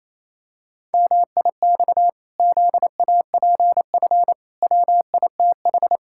Morse code, heard as a keyed tone, about 28 wpm, 700 Hz. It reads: MIX ZAPF WITH
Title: Pangram Challenge v2 - 28wpm